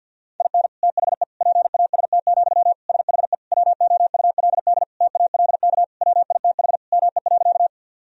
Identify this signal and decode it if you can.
Morse code, heard as a keyed tone, about 50 wpm, 700 Hz. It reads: IN THE PAST, SHE WOULD TALK WITH ME.
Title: Sentences from Top 1000 Words - Review Part 3 - 50wpm